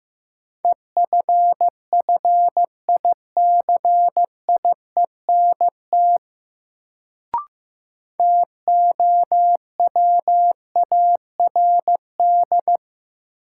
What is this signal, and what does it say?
Morse code, heard as a keyed tone, about 15 wpm, 700 Hz. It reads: EFFICIENT TOWARD
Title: Sentences from Top 1300 Words - New Word Drill Alt - 15wpm